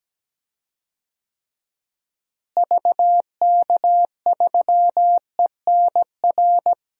Morse code, heard as a keyed tone, about 17 wpm, 700 Hz. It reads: VK3ENR
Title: DXCC-100 Call Signs - 6 Characters - Repeated 3x - 17wpm